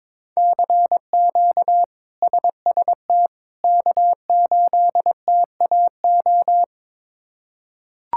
Morse code, heard as a keyed tone, about 22 wpm, 700 Hz. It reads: CQ SST K8TAO
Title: K1USN SST QSO Practice - 22wpm